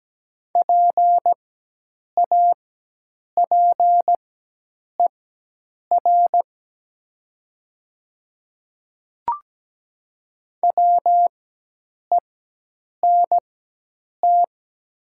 Morse code, heard as a keyed tone, about 17 wpm, spaced as for 8 wpm, 700 Hz. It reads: PAPER WENT
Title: 4x Character Spacing - Top 300 Words - 17wpm